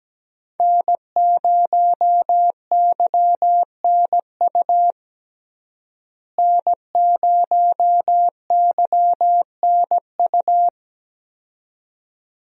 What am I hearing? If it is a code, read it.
Morse code, heard as a keyed tone, about 17 wpm, 700 Hz. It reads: N0YNU N0YNU
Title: USA Call Signs - 5 Characters - Repeated 2x - 17wpm